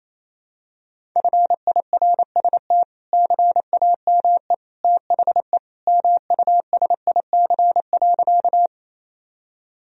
Morse code, heard as a keyed tone, about 28 wpm, 700 Hz. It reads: FIRST CAME THE MUSIC.
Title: Sentences from Top 1000 Words - Review Part 1 - 28wpm